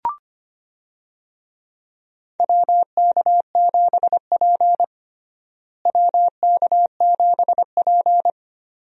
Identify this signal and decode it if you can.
Morse code, heard as a keyed tone, about 25 wpm, 700 Hz. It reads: WK7P WK7P
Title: USA Call Signs - 4 Characters - Repeated 2x - 25wpm